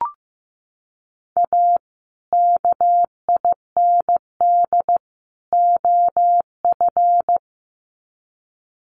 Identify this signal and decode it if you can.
Morse code, heard as a keyed tone, about 15 wpm, 700 Hz. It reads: A KIND OF
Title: Sets of 3 Words 15wpm